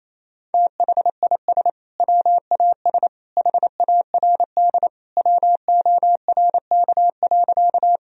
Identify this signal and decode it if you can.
Morse code, heard as a keyed tone, about 28 wpm, 700 Hz. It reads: THIS WAS HARD WORK.
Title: Sentences from Top 1000 Words - Review Part 1 - 28wpm